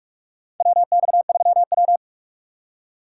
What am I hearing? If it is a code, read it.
Morse code, heard as a keyed tone, about 46 wpm, 700 Hz. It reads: WX3W
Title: USA Call Signs - 4 Characters - 2x Character Spacing - 45wpm